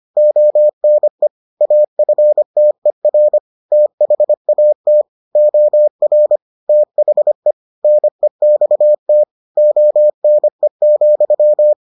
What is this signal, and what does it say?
Morse code, heard as a keyed tone, about 25 wpm, 600 Hz. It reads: ONE AFTER THAT OR THE NEXT ONE,